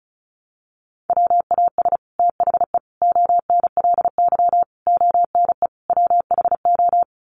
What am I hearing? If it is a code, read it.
Morse code, heard as a keyed tone, about 34 wpm, 700 Hz. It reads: WAS THE ONLY ONE WHO
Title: Sets of 5 Words Encore2 35wpm